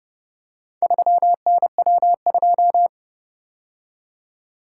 Morse code, heard as a keyed tone, about 29 wpm, 700 Hz. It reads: 3NW2